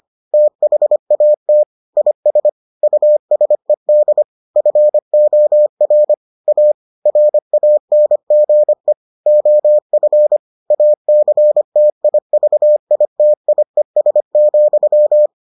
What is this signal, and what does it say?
Morse code, heard as a keyed tone, about 25 wpm, 600 Hz. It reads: THAT IS USED FOR A RANGE OF ACTIVITIES,